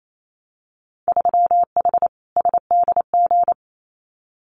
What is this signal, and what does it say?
Morse code, heard as a keyed tone, about 28 wpm, 700 Hz. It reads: 3H SDG
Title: Winter Field Day Exchanges Encore 28wpm